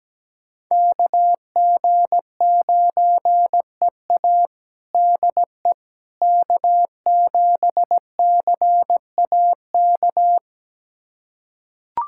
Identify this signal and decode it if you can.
Morse code, heard as a keyed tone, about 17 wpm, 700 Hz. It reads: KG9EA DE K7CAK
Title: USA Callsign DE Callsign - 5 Characters - 17wpm